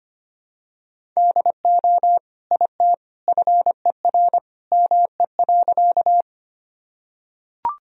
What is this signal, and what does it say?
Morse code, heard as a keyed tone, about 25 wpm, 700 Hz. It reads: DO IT FER ME.
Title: Sentences from Top 200 Words and Abbreviations 25wpm